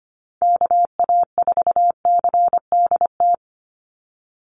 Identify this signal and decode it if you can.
Morse code, heard as a keyed tone, about 25 wpm, 690 Hz. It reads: KA4CDT
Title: USA Call Signs 25wpm